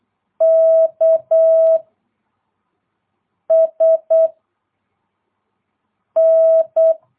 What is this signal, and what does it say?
Morse code, heard as a keyed tone, about 8 wpm, 660 Hz. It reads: K S N
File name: jh8cbh / 2022cw